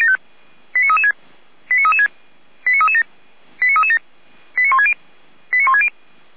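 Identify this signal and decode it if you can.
Five-tone selective call (ZVEI), heard as quick runs of five tones, tones of about 70 ms. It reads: 89336 89336 89337 89337 89160 89160
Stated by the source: ZVEI five-tone selective call received with an SDR.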